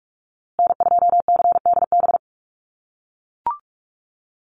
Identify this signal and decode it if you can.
Morse code, heard as a keyed tone, about 44 wpm, 690 Hz. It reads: N2CDB
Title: USA Call Signs Repeated 2x 45wpm